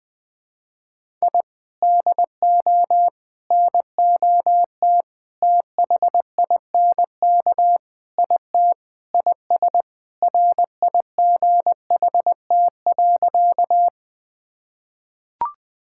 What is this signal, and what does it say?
Morse code, heard as a keyed tone, about 20 wpm, 700 Hz. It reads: I DO NOT THINK IT IS RIGHT.
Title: Sentences with 4-Word Phrases - 20wpm